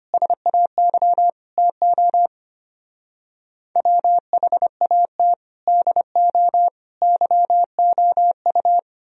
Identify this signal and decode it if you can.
Morse code, heard as a keyed tone, about 30 wpm, 700 Hz. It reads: SAY TO WHAT DO YOU
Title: Speed-Racing - Sets of 5 Words 25wpm